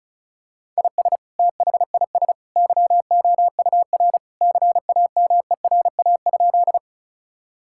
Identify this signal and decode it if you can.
Morse code, heard as a keyed tone, about 34 wpm, 700 Hz. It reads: IS THIS YOUR CAMERA?
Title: Sentences from Top 1000 Words - Review Part 5 - 35wpm